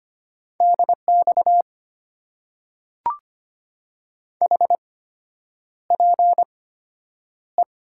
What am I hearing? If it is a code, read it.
Morse code, heard as a keyed tone, about 25 wpm, spaced as for 9 wpm, 700 Hz. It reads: DX HPE
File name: Top 90 QSO Elements - 8x Character Spacing - 25wpm